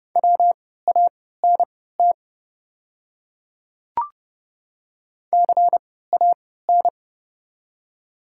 Morse code, heard as a keyed tone, about 31 wpm, spaced as for 17 wpm, 700 Hz. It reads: WANT CAN
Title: 3x Character Spacing - Top 200 Words - 30wpm